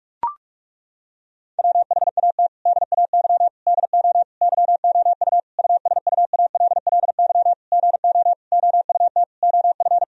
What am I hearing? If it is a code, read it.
Morse code, heard as a keyed tone, about 41 wpm, 700 Hz. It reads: WHAT DAY DO YOU USUALLY GO OUT OF
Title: Sentences from Top 1000 Words - Review Part 3 - 45wpm